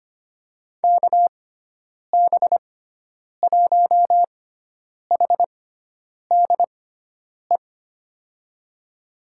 Morse code, heard as a keyed tone, about 25 wpm, spaced as for 9 wpm, 700 Hz. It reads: KB1HDE
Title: USA Call Signs - 6 Characters - 6x Character Spacing - 25wpm